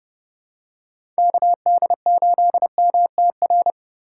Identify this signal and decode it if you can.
Morse code, heard as a keyed tone, about 30 wpm, 700 Hz. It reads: KD8MTR
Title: USA Call Signs - 6 Characters - 2x Character Spacing - 30wpm